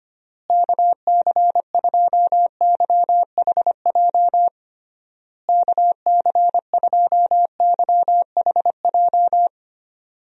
Morse code, heard as a keyed tone, about 25 wpm, 700 Hz. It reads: KC2YHJ KC2YHJ
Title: USA Call Signs - 6 Characters - Repeated 3x - 25wpm